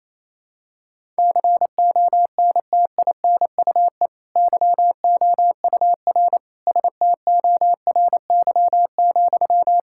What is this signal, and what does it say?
Morse code, heard as a keyed tone, about 28 wpm, 690 Hz. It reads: CONTINUE YOUR STORY,
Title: Sentences from Top 1000 Words - Review Part 6 - 28wpm